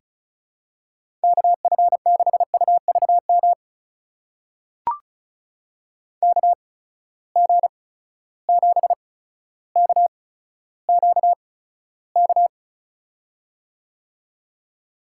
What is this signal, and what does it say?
Morse code, heard as a keyed tone, about 36 wpm, spaced as for 15 wpm, 700 Hz. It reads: KF6UVM KG7KQK
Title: USA Call Signs - 6 Characters - 8x Character Spacing - 35wpm